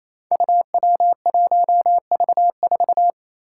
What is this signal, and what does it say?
Morse code, heard as a keyed tone, about 28 wpm, 700 Hz. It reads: UW1V4
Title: Five Challenging Letter-Number Combinations 28wpm